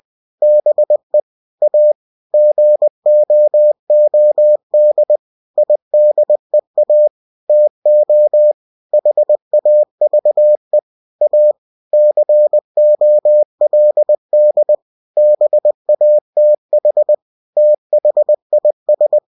Morse code, heard as a keyed tone, about 20 wpm, 600 Hz. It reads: BE A GOOD IDEA TO HAVE A COLD BATH THIS
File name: cw-books / winnie-the-pooh / winnie-20wpm-page-0071